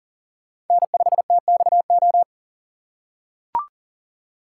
Morse code, heard as a keyed tone, about 40 wpm, 700 Hz. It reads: N5TXO
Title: USA Call Signs - 5 Characters - 2x Character Spacing - 40wpm